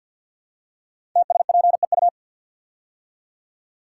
Morse code, heard as a keyed tone, about 53 wpm, 700 Hz. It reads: TSPEU
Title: Code Groups - 5 Letter:Numbers - 50wpm